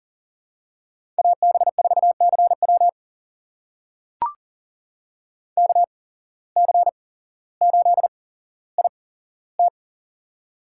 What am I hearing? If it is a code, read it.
Morse code, heard as a keyed tone, about 40 wpm, spaced as for 17 wpm, 700 Hz. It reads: AB4CW KC8IT
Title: USA Call Signs - 5 Characters - 8x Character Spacing - 40wpm